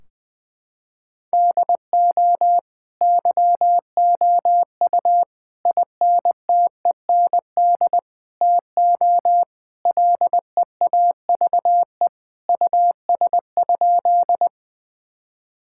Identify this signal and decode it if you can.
Morse code, heard as a keyed tone, about 20 wpm, 700 Hz. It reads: DO YOU INTEND TO LEAVE US?
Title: Sentences from Top 1600 Words 20wpm